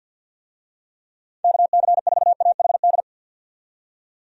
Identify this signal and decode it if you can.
Morse code, heard as a keyed tone, about 47 wpm, 700 Hz. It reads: KC4AHD
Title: USA Call Signs - 6 Characters - 50wpm